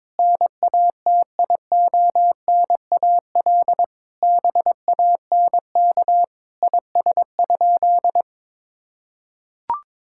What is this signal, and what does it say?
Morse code, heard as a keyed tone, about 22 wpm, 700 Hz. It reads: NATIONAL BANK IS?
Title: Sentences from Top 1000 Words - Review Part 6 - 22wpm